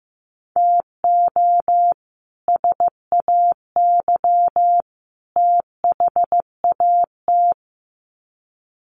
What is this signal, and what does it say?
Morse code, heard as a keyed tone, about 15 wpm, 700 Hz. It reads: TO SAY THAT
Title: Sets of 3 Words 15wpm